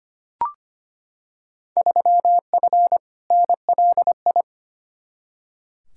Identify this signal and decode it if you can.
Morse code, heard as a keyed tone, about 25 wpm, 700 Hz. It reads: 3F NLI
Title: Field Day Exchanges Encore - 25wpm